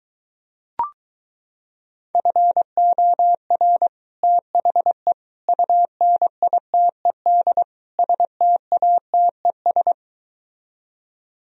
Morse code, heard as a keyed tone, about 23 wpm, 700 Hz. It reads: FOR THE UNITED STATES